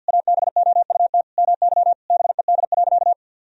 Morse code, heard as a keyed tone, about 50 wpm, 700 Hz. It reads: ABOUT MY BED.
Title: Sentences from Top 1000 Words - Review Part 4 - 50wpm